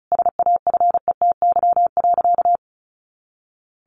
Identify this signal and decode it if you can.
Morse code, heard as a keyed tone, about 34 wpm, 700 Hz. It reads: SAFETY.
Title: A Sound Of Thunder 35wpm